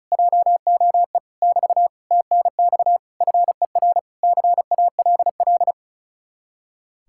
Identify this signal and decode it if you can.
Morse code, heard as a keyed tone, about 47 wpm, spaced as for 42 wpm, 700 Hz. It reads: JOE <BT> TNX FER CALL
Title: Speed-Racing Common QSO Phrases - 35wpm